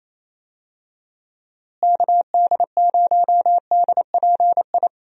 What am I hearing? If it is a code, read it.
Morse code, heard as a keyed tone, about 28 wpm, 700 Hz. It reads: KD0DPI
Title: USA Call Signs - 6 Characters - Repeated 2x - 28wpm